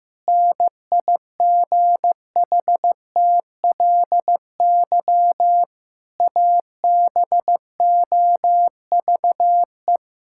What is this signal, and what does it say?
Morse code, heard as a keyed tone, about 15 wpm, 700 Hz. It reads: NIGHTLY ABOVE
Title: Pangram Challenge v2 - 15wpm